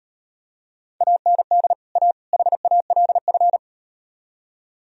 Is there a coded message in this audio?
Yes: Morse code from 1.0 to 3.6 s, heard as a keyed tone, about 38 wpm, 700 Hz. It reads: AND A HALF